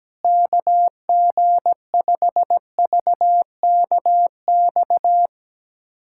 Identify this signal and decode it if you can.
Morse code, heard as a keyed tone, about 17 wpm, 700 Hz. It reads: KG5VKX
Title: USA Callsign DE Callsign - 6 Characters - 17wpm